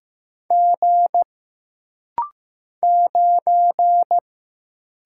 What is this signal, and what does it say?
Morse code, heard as a keyed tone, about 15 wpm, 700 Hz. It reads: G 9
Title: Single Letter-Number Alt - Rapid-Fire 15wpm